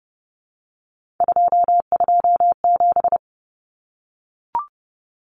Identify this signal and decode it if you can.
Morse code, heard as a keyed tone, about 30 wpm, 700 Hz. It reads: 227
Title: Common Cut Numbers - 3-Digit 30wpm